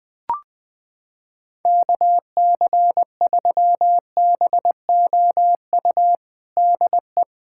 Morse code, heard as a keyed tone, about 21 wpm, 700 Hz. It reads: KC3BOU DE
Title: USA Callsign DE Callsign 20wpm